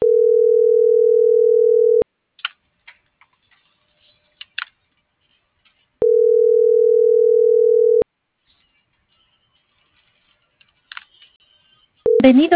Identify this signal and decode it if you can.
Ringback tone (North America), 440 + 480 Hz.